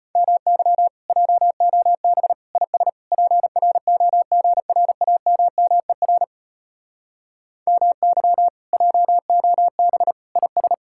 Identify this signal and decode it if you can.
Morse code, heard as a keyed tone, about 33 wpm, 700 Hz. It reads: MY JOB IS PROGRAMMER MY JOB IS